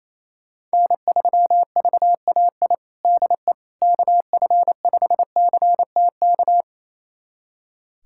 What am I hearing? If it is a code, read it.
Morse code, heard as a keyed tone, about 28 wpm, 700 Hz. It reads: N3VAI DE KF5CTK
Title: USA Callsign DE Callsign 28wpm